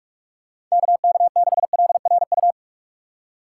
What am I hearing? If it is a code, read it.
Morse code, heard as a keyed tone, about 46 wpm, 700 Hz. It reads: KK6LRU
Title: USA Call Signs - 6 Characters - 2x Character Spacing - 45wpm